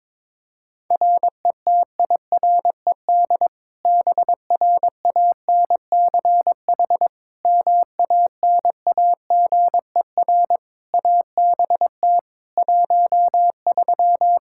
Morse code, heard as a keyed tone, about 21 wpm, 700 Hz. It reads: RETIRED BRANCH MANAGER ABT 13
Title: K1USN SST QSO Practice 22wpm